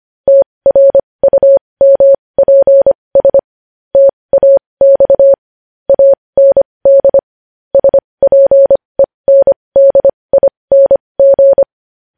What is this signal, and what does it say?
Morse code, heard as a keyed tone, about 25 wpm, spaced as for 20 wpm, 550 Hz. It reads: TRUMPS TAX AND SPENDING
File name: podcast / News headlines-1x-25x20